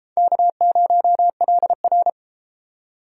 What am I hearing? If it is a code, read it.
Morse code, heard as a keyed tone, about 30 wpm, 700 Hz. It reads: K0LR